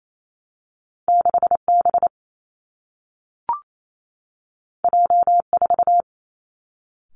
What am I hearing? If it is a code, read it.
Morse code, heard as a keyed tone, about 28 wpm, 700 Hz. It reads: 6 B J 4
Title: Two Challenging Letter-Number Combinations 28wpm